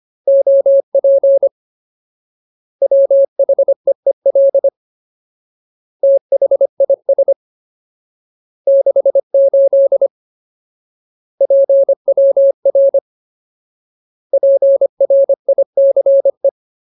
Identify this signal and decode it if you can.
Morse code, heard as a keyed tone, about 25 wpm, 550 Hz. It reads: OP WHEEL THIS 68 PWR PRICE
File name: Ultimate Word List / 25WPM Ultimate